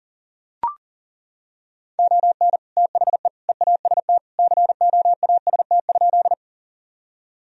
Morse code, heard as a keyed tone, about 40 wpm, 700 Hz. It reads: ON THE EAST COAST?
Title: Sentences from Top 1000 Words - Review Part 4 - 40wpm